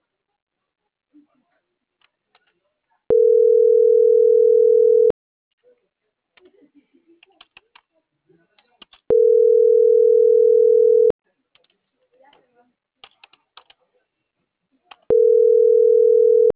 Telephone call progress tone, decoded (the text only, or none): ringback tone